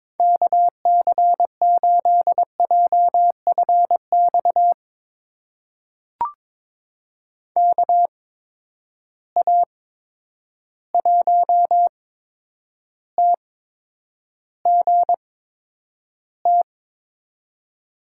Morse code, heard as a keyed tone, about 21 wpm, spaced as for 9 wpm, 700 Hz. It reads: KC8JFX KA1TGT
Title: USA Call Signs - 6 Characters - 8x Character Spacing - 22wpm